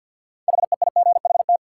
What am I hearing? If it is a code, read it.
Morse code, heard as a keyed tone, about 52 wpm, 700 Hz. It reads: HEIGHT
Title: Sentences from Top 1600 Words - New Word Drill - 50wpm